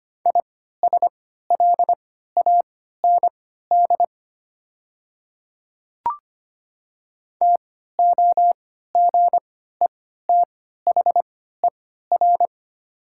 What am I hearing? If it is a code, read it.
Morse code, heard as a keyed tone, about 25 wpm, spaced as for 14 wpm, 700 Hz. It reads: ISLAND TOGETHER
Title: 3x Character Spacing - Top 500 Words - 25wpm